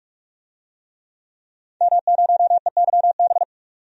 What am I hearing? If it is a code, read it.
Morse code, heard as a keyed tone, about 45 wpm, 700 Hz. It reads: M0EYB
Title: DXCC-100 Call Signs - 5 Characters - 45wpm